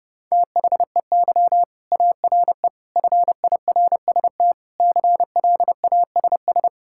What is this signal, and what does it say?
Morse code, heard as a keyed tone, about 32 wpm, 700 Hz. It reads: THEY ARE FIRST CLASS
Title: Sentences from Top 1000 Words - Review Part 5 - 30wpm